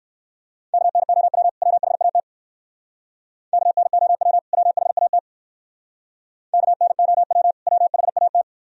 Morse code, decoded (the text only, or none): KNOW WHAT KNOW WHAT KNOW WHAT